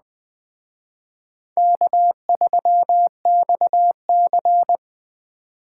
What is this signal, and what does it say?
Morse code, heard as a keyed tone, about 20 wpm, 700 Hz. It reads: K3XC
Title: Speed Racing - USA 1x2 Call Signs 17wpm